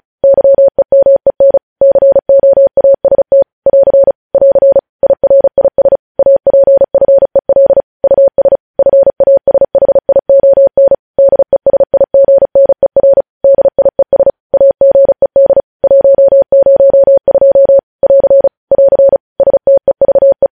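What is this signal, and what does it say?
Morse code, heard as a keyed tone, about 36 wpm, 550 Hz. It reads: YEMEN COAST <AR> <AR> IRIS APFEL US FASHION DESIGNER DIES AGED 102 <AR> <AR> STEVE
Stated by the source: Morse practice at 35 wpm sending news headlines.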